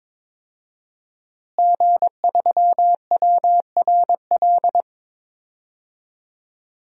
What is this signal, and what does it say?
Morse code, heard as a keyed tone, about 22 wpm, 700 Hz. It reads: G3WRL
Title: DXCC-100 Call Signs - 5 Characters - Repeated 2x - 22wpm